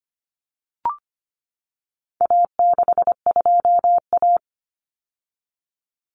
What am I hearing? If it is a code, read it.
Morse code, heard as a keyed tone, about 25 wpm, 700 Hz. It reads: A62A
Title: Common Cut Numbers - 4-Digit 25wpm